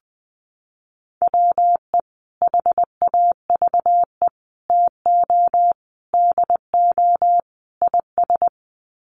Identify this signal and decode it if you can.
Morse code, heard as a keyed tone, about 20 wpm, 700 Hz. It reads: WE HAVE TO DO IS
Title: Sets of 5 Words Encore 20wpm